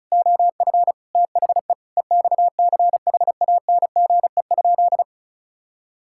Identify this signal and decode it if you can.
Morse code, heard as a keyed tone, about 34 wpm, 700 Hz. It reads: OF THE EXCHANGE?
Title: Sentences from Top 1000 Words - Review Part 6 - 35wpm